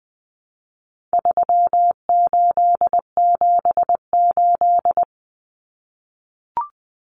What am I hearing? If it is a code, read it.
Morse code, heard as a keyed tone, about 20 wpm, 700 Hz. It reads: 3878
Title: Numbers - 4-Digit 20wpm